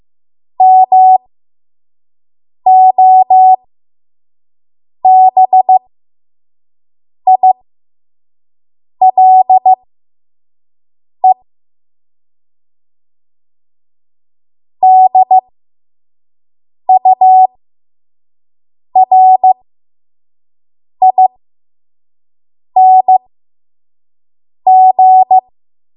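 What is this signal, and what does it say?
Morse code, heard as a keyed tone, about 15 wpm, spaced as for 5 wpm, 750 Hz. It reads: MOBILE DURING
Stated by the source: 5 WPM Morse code practice.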